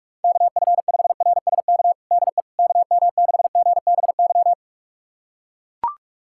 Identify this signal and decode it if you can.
Morse code, heard as a keyed tone, about 42 wpm, 700 Hz. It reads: KF5RSK DE KM6GBY
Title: USA Callsign DE Callsign - 6 Characters - 45wpm